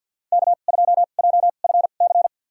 Morse code, heard as a keyed tone, about 46 wpm, spaced as for 35 wpm, 700 Hz. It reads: K2JFC
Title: USA Call Signs - 5 Characters - 2x Character Spacing - 50wpm